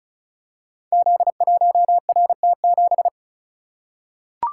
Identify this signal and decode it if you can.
Morse code, heard as a keyed tone, about 36 wpm, 700 Hz. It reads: Z1RT7
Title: Code Groups - 5 Letter:Numbers - 35wpm